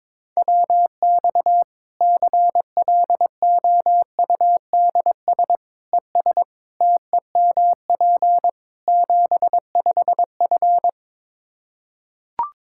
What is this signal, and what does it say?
Morse code, heard as a keyed tone, about 22 wpm, 700 Hz. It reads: WX CLOUDS ES TEMP 75F
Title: K1USN SST QSO Practice 22wpm